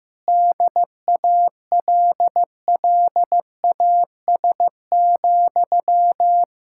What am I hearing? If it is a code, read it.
Morse code, heard as a keyed tone, about 15 wpm, 700 Hz. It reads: DALLAS,
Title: Speed-Racing Common QSO Phrases Encore - 15wpm